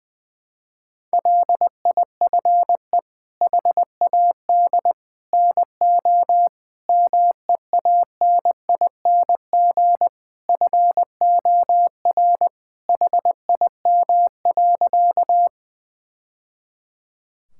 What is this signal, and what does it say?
Morse code, heard as a keyed tone, about 20 wpm, 700 Hz. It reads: LIFE HAD NO MEANING FOR HIM.
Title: Sentences from Top 1000 Words - Review Part 6 - 20wpm